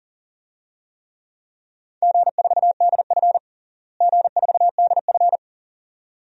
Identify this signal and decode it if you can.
Morse code, heard as a keyed tone, about 40 wpm, 700 Hz. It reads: G4DF G4DF
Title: DXCC-100 Call Signs - 4 Characters - Repeated 2x - 40wpm